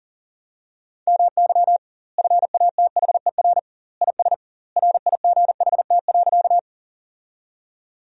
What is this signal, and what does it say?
Morse code, heard as a keyed tone, about 40 wpm, 700 Hz. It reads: MY FATHER IS RIGHT.
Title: 2x Word-Spacing - Sentences from Top 300 Words - 40wpm